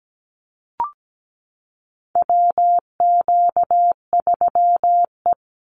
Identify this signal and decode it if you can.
Morse code, heard as a keyed tone, about 17 wpm, 700 Hz. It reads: WQ3E
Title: Call Signs Repeated 2x - 17wpm